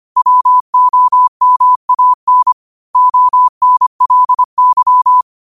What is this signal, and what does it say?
Morse code, heard as a keyed tone, about 25 wpm, 1000 Hz. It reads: WOMAN ONLY